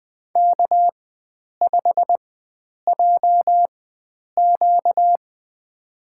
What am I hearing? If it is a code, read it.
Morse code, heard as a keyed tone, about 20 wpm, spaced as for 9 wpm, 700 Hz. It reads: K5JQ
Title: USA Call Signs - 4x Character Spacing - 20wpm